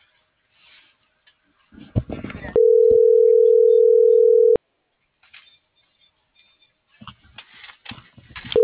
Ringback tone (North America), 440 + 480 Hz.